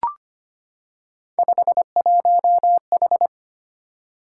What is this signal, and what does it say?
Morse code, heard as a keyed tone, about 25 wpm, 700 Hz. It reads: 51H
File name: Three Letter-Number Combinations - 25wpm